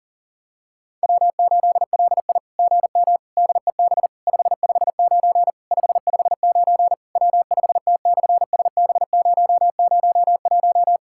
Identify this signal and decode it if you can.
Morse code, heard as a keyed tone, about 37 wpm, 700 Hz. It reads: W8LI GM DEB 559 559 W5T/SB001
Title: SOTA QSO Practice 40wpm